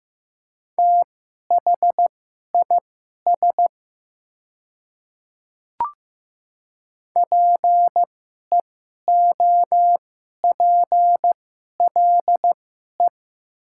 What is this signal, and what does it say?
Morse code, heard as a keyed tone, about 15 wpm, spaced as for 11 wpm, 700 Hz. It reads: THIS PEOPLE
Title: 2x Character Spacing - Top 500 Words - 15wpm